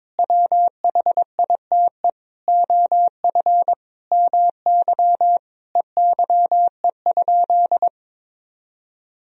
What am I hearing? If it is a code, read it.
Morse code, heard as a keyed tone, about 22 wpm, 700 Hz. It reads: WHITE OF MY EYE?
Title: Sentences from Top 500 Words and Abbreviations 22wpm